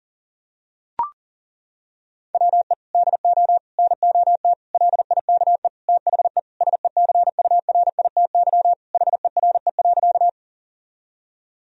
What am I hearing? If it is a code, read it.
Morse code, heard as a keyed tone, about 40 wpm, 700 Hz. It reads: WE DO NOT LIKE THE SECURITY HERE.
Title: Sentences from Top 1000 Words - Review Part 4 - 40wpm